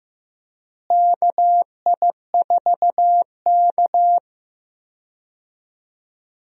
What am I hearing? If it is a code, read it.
Morse code, heard as a keyed tone, about 15 wpm, 700 Hz. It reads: KI4K
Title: USA Call Signs - 4 Characters - Repeated 3x - 15wpm